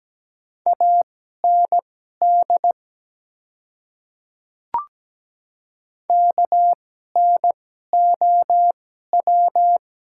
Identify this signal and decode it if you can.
Morse code, heard as a keyed tone, about 17 wpm, spaced as for 12 wpm, 700 Hz. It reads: AND KNOW
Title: 2x Character Spacing - Top 100 Words - 17wpm